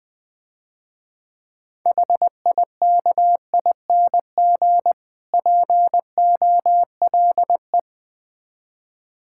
Morse code, heard as a keyed tone, about 22 wpm, 700 Hz. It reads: HIKING POLE
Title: Difficult Words - Speed-Racing with Hints - 20wpm